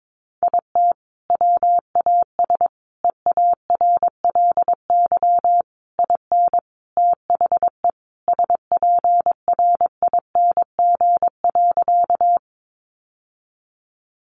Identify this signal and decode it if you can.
Morse code, heard as a keyed tone, about 22 wpm, 700 Hz. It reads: IT WAS EARLY IN THE SPRING.
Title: Sentences from Top 1200 Words 22wpm